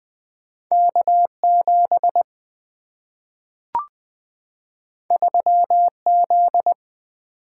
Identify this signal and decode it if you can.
Morse code, heard as a keyed tone, about 20 wpm, 700 Hz. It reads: K7 3Z
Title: Single-Letter Single-Number 20wpm